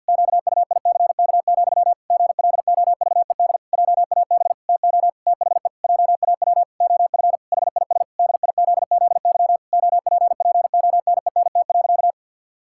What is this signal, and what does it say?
Morse code, heard as a keyed tone, about 52 wpm, 700 Hz. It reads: QUICK, GLOVED JAB TO THE JAW OF HIS DIZZY OPPONENT.
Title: Pangram Challenge v2 - 50wpm